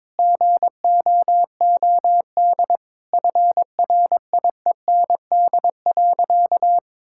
Morse code, heard as a keyed tone, about 21 wpm, 700 Hz. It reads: GOOD FRIEND.
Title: Sentences from Top 1600 Words 22wpm